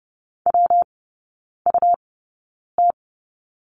Morse code, heard as a keyed tone, about 30 wpm, spaced as for 9 wpm, 700 Hz. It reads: WUT
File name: Lesson 19e - Letter w - Three Characters 30wpm